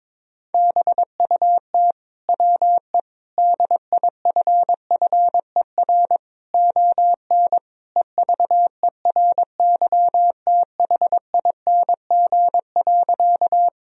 Morse code, heard as a keyed tone, about 22 wpm, 700 Hz. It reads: BUT WE DIFFER ON EVERYTHING.
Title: Sentences from Top 1000 Words - Review Part 1 - 22wpm